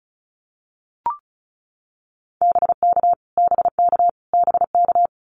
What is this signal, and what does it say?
Morse code, heard as a keyed tone, about 34 wpm, 700 Hz. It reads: BK BK BK
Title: Abbreviations Repeated 3x - 35wpm